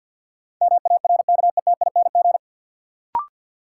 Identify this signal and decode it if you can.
Morse code, heard as a keyed tone, about 52 wpm, 700 Hz. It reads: MARKETING
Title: Sentences from Top 1000 Words - Review Part 5 - New Word Drill - 50wpm